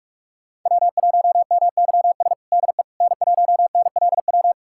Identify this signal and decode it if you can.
Morse code, heard as a keyed tone, about 43 wpm, 700 Hz. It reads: W1MYS DE N1NLW
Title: USA Callsign DE Callsign - 5 Characters - 45wpm